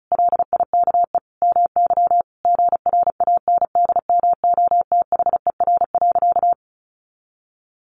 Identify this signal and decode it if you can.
Morse code, heard as a keyed tone, about 34 wpm, 700 Hz. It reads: LIKE MY GRANDMOTHER.
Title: Sentences from Top 1200 Words 35wpm